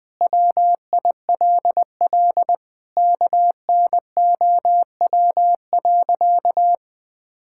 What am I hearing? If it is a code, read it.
Morse code, heard as a keyed tone, about 21 wpm, 700 Hz. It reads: WILL KNOW.
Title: Sentences with 4-Word Phrases - 20wpm